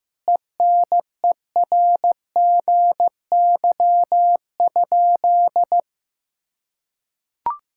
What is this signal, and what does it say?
Morse code, heard as a keyed tone, about 15 wpm, 700 Hz. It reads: ENERGY?
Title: Sentences from Top 1000 Words - Review Part 6 - 15wpm